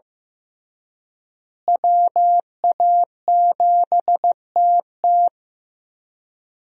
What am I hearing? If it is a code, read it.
Morse code, heard as a keyed tone, about 15 wpm, 700 Hz. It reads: WA7TT
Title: USA Call Signs - 5 Characters - Repeated 3x - 15wpm